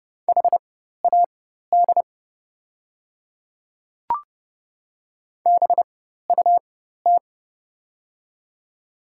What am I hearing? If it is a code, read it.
Morse code, heard as a keyed tone, about 30 wpm, spaced as for 14 wpm, 700 Hz. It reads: HAD BUT
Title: 4x Character Spacing - Top 300 Words - 30wpm